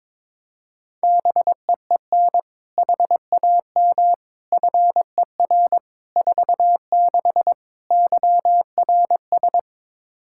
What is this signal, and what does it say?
Morse code, heard as a keyed tone, about 22 wpm, 700 Hz. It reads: BEEN HAM FER 46 YRS
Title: K1USN SST QSO Practice 22wpm